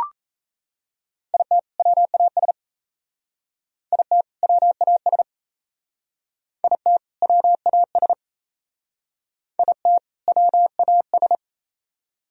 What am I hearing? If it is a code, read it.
Morse code, heard as a keyed tone, about 34 wpm, 700 Hz. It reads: IT WAS IT WAS IT WAS IT WAS
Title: Speed-Racing - Sets of 2 Words 28wpm